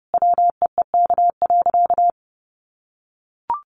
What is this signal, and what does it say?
Morse code, heard as a keyed tone, about 30 wpm, 700 Hz. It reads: WEEK.
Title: Easy Phrases 30wpm